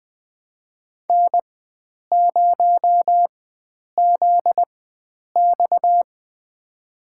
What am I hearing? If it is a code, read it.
Morse code, heard as a keyed tone, about 20 wpm, spaced as for 9 wpm, 700 Hz. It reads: N0ZX
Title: USA Call Signs - 4 Characters - 4x Character Spacing - 20wpm